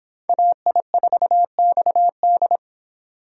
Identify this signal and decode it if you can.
Morse code, heard as a keyed tone, about 25 wpm, 700 Hz. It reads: AI4XD